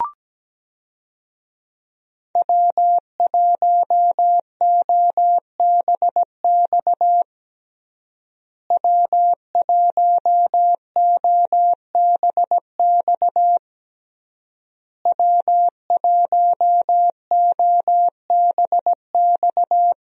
Morse code, heard as a keyed tone, about 17 wpm, 700 Hz. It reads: W1OBX W1OBX W1OBX
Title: USA Call Signs - 5 Characters - Repeated 3x - 17wpm